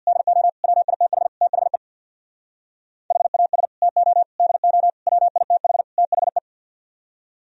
Morse code, heard as a keyed tone, about 65 wpm, 700 Hz. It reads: DO WITH THE HAS TO DO WITH THE